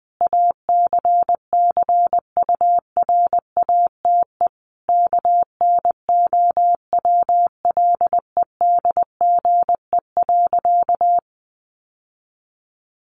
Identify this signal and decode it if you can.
Morse code, heard as a keyed tone, about 19 wpm, 700 Hz. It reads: ACCURATE KNOWLEDGE.
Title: Sentences from Top 1200 Words 20wpm